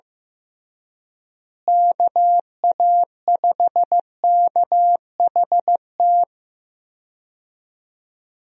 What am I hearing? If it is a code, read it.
Morse code, heard as a keyed tone, about 15 wpm, 700 Hz. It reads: KA5KHT
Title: USA Call Signs - 6 Characters - Repeated 3x - 15wpm